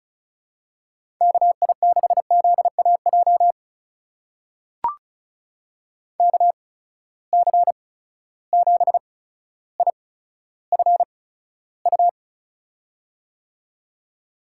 Morse code, heard as a keyed tone, about 36 wpm, spaced as for 15 wpm, 700 Hz. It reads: KI6ZAJ KC7IFU
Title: USA Call Signs - 6 Characters - 8x Character Spacing - 35wpm